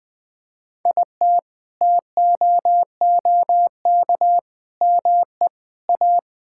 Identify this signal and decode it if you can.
Morse code, heard as a keyed tone, about 21 wpm, 700 Hz. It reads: IT TOOK ME A
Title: Sets of 6 Words - 20wpm